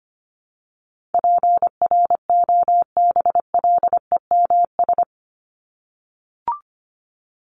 Morse code, heard as a keyed tone, about 25 wpm, 700 Hz. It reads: PROBLEMS
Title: ES:S Suffix - Top 100 Words with Base Words 25wpm